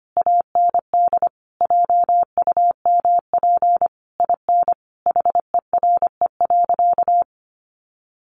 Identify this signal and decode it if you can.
Morse code, heard as a keyed tone, about 25 wpm, 700 Hz. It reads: AND JUMP IN HERE.
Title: Sentences from Top 1300 Words 25wpm